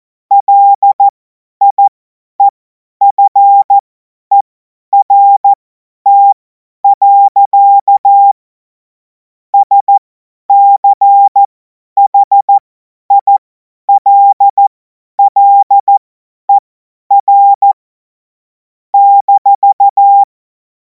Morse code, heard as a keyed tone, about 14 wpm, spaced as for 10 wpm, 800 Hz. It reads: LIEFERT. SCHILLER -
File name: Morsekurs / Witze / Witz 10. März 2026 50 F70